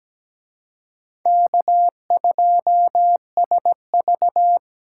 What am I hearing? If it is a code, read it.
Morse code, heard as a keyed tone, about 20 wpm, 700 Hz. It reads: K2SV
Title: Speed Racing - USA 1x2 Call Signs 17wpm